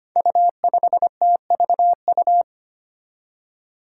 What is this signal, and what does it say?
Morse code, heard as a keyed tone, about 25 wpm, 700 Hz. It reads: U5TVU
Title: Five Challenging Letter-Number Combinations 25wpm